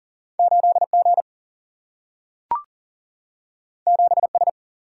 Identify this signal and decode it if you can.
Morse code, heard as a keyed tone, about 40 wpm, 700 Hz. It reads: 8G 7S
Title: Single-Letter Single-Number 40wpm